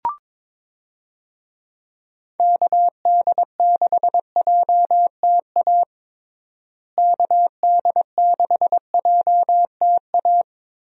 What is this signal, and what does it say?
Morse code, heard as a keyed tone, about 22 wpm, 700 Hz. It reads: KD6JTA KD6JTA
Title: USA Call Signs - 6 Characters - Repeated 3x - 22wpm